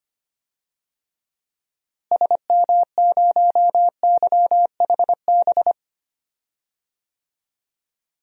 Morse code, heard as a keyed tone, about 25 wpm, 700 Hz. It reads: SM0YHB
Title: DXCC-100 Call Signs - 6 Characters - Repeated 2x - 25wpm